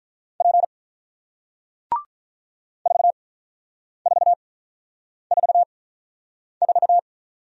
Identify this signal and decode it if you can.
Morse code, heard as a keyed tone, about 48 wpm, spaced as for 9 wpm, 690 Hz. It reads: P 4444